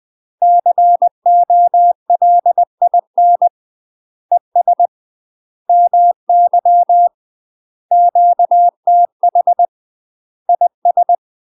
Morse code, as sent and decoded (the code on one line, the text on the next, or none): -.-. --- .-.. .. -. / . ... / -- -.-- / --.- - .... / .. ...
COLIN ES MY QTH IS